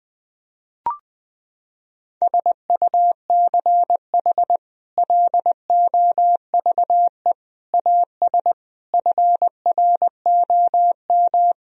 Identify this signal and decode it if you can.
Morse code, heard as a keyed tone, about 20 wpm, 700 Hz. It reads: SUCH LOVE AS FROM
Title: Sentences from Top 1000 Words - Review Part 1 - 20wpm